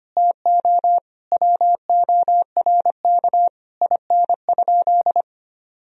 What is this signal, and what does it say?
Morse code, heard as a keyed tone, about 25 wpm, 700 Hz. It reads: TO WORK IN?
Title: Sentences from Top 1000 Words - Review Part 4 - 25wpm